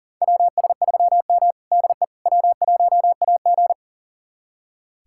Morse code, heard as a keyed tone, about 40 wpm, 700 Hz. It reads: WS3M DE W1AG
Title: USA Callsign DE Callsign - 4 Characters - 40wpm